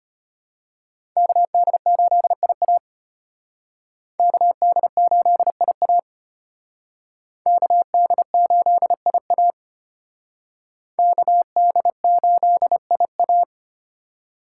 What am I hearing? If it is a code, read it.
Morse code, heard as a keyed tone, about 32 wpm, 700 Hz. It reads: KD8IA KD8IA KD8IA KD8IA